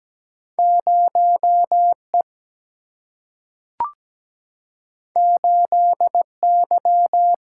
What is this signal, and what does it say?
Morse code, heard as a keyed tone, about 17 wpm, 700 Hz. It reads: 0E 8Y
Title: Single-Letter Single-Number 17wpm